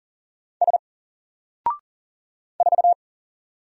Morse code, heard as a keyed tone, about 40 wpm, 700 Hz. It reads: S 4
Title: Single Character (Letter:Number:Punctuation:Prosign) 40wpm